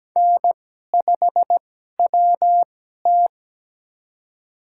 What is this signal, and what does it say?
Morse code, heard as a keyed tone, about 17 wpm, spaced as for 12 wpm, 700 Hz. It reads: N5WT
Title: USA Call Signs - 4 Characters - 2x Character Spacing - 17wpm